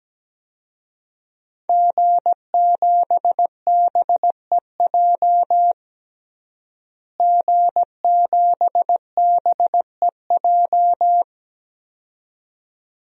Morse code, heard as a keyed tone, about 17 wpm, 700 Hz. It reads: G7BEJ G7BEJ
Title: DXCC-100 Call Signs - 5 Characters - Repeated 2x - 17wpm